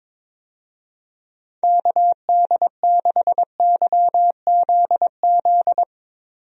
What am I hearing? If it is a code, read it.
Morse code, heard as a keyed tone, about 22 wpm, 700 Hz. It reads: KD6YZZ
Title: USA Call Signs - 6 Characters - Repeated 3x - 22wpm